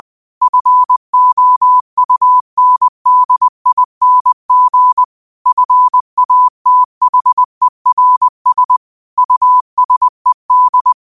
Morse code, heard as a keyed tone, about 20 wpm, 1000 Hz. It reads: FOUNDING FATHERS USED
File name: qotdfiles / 20wpm